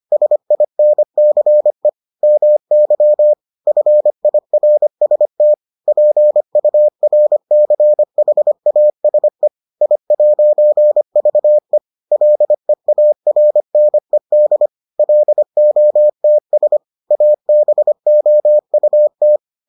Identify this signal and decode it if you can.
Morse code, heard as a keyed tone, about 25 wpm, 600 Hz. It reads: SINCE MY FIRST PURCHASE I'VE LEARNED LOTS ABOUT